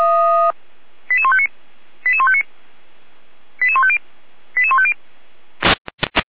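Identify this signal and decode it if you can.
Five-tone selective call (ZVEI), heard as quick runs of five tones, tones of about 70 ms. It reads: 88159 88159 88150 88150
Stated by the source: ZVEI five-tone call received by SDR.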